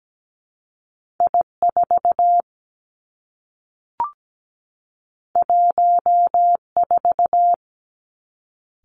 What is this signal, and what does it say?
Morse code, heard as a keyed tone, about 17 wpm, 700 Hz. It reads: I4 14
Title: Two Challenging Letter-Number Combinations 17wpm